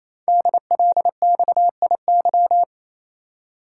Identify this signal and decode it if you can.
Morse code, heard as a keyed tone, about 27 wpm, 700 Hz. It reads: DLXIY
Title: Code Groups - 5 Letters - 28wpm